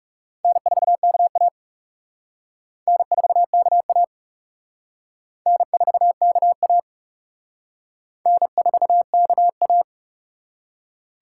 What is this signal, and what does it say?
Morse code, heard as a keyed tone, about 36 wpm, 700 Hz. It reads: N4KA N4KA N4KA N4KA
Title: Speed Racing - USA 1x2 Call Signs 30wpm